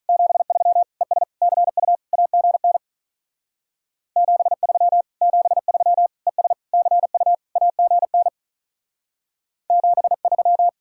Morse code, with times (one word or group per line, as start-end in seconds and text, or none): 0.1-0.8 73
1.0-1.2 ES
1.4-1.9 CU
2.1-2.8 AGN
4.2-5.0 73
5.2-6.1 73
6.3-6.5 ES
6.7-7.3 CU
7.6-8.3 AGN
9.7-10.7 73